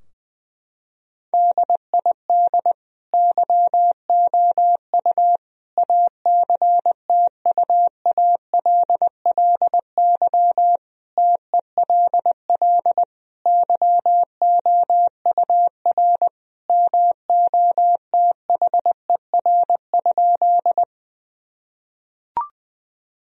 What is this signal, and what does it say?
Morse code, heard as a keyed tone, about 20 wpm, 700 Hz. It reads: DID YOU ACTUALLY TELL YOUR MOTHER?
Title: Sentences from Top 1000 Words - Review Part 4 - 20wpm